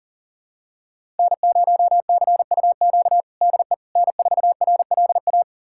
Morse code, heard as a keyed tone, about 40 wpm, 700 Hz. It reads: N0CUQ DE N4RLA
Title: USA Callsign DE Callsign - 5 Characters - 40wpm